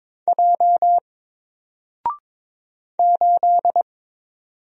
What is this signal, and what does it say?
Morse code, heard as a keyed tone, about 22 wpm, 700 Hz. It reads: J 8
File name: Single-Letter-Number-22wpm